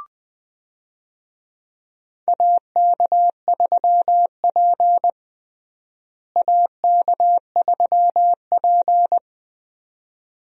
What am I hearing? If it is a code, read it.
Morse code, heard as a keyed tone, about 20 wpm, 700 Hz. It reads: AK3P AK3P
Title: USA Call Signs - 4 Characters - Repeated 2x - 20wpm